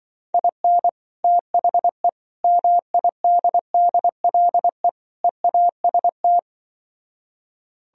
Morse code, heard as a keyed tone, about 25 wpm, 700 Hz. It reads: IN THE MIDDLE EAST
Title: Speed-Racing - Sets of 4 Words 20wpm